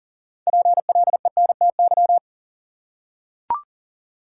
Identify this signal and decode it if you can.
Morse code, heard as a keyed tone, about 40 wpm, 700 Hz. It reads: PLENTY
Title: Six Letter Words - 40wpm